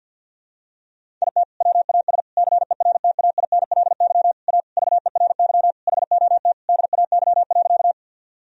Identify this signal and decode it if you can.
Morse code, heard as a keyed tone, about 51 wpm, 700 Hz. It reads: IT WAS CERTAINLY A VERY HOT DAY.
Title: Sentences from Top 1000 Words - Review Part 3 - 50wpm